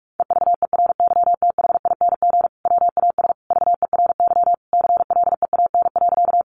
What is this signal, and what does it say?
Morse code, heard as a keyed tone, about 46 wpm, 700 Hz. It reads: EVERYTHING WAS VERY CLEAN.
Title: Sentences from Top 1100 Words 45wpm